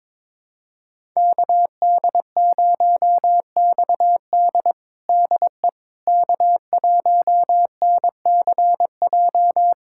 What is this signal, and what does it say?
Morse code, heard as a keyed tone, about 22 wpm, 700 Hz. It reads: KD0XD DE K1NCJ
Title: USA Callsign DE Callsign - 5 Characters - 22wpm